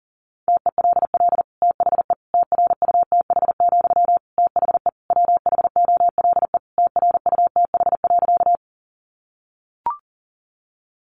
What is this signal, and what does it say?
Morse code, heard as a keyed tone, about 40 wpm, 700 Hz. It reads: TELL THE TRUTH, THE WHOLE TRUTH.
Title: Sentences from Top 800 Words 40wpm